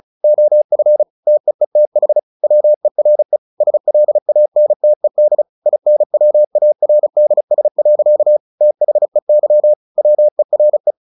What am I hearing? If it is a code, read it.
Morse code, heard as a keyed tone, about 34 wpm, 600 Hz. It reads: OF TEETH WERE SLANTED INWARDS. THEY WERE
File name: the old man and the sea 2 35wpm